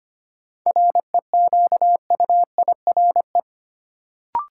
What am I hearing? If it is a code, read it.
Morse code, heard as a keyed tone, about 25 wpm, 700 Hz. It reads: REQUIRE
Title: Sentences from Top 900 Words - New Word Drill - 25wpm